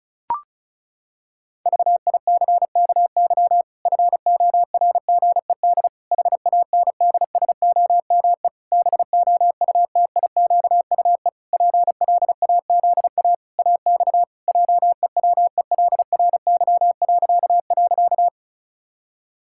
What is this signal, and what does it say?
Morse code, heard as a keyed tone, about 36 wpm, 700 Hz. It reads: VICKY FORGED HANDSOME BOUTIQUE PLAZA AX JEWELRY..
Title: Pangram Challenge v2 - 35wpm